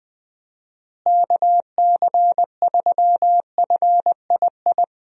Morse code, heard as a keyed tone, about 20 wpm, 700 Hz. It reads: KC3FII
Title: USA Call Signs - 6 Characters - Repeated 2x - 20wpm